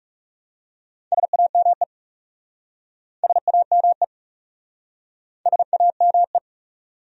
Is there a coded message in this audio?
Yes: Morse code from 1.1 to 6.4 s, heard as a keyed tone, about 40 wpm, 700 Hz. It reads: SAME SAME SAME